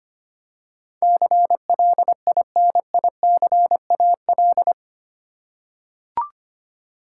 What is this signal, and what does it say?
Morse code, heard as a keyed tone, about 25 wpm, 700 Hz. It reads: CLINICAL
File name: 8 Letter Words - First 3 Letters Given - 25wpm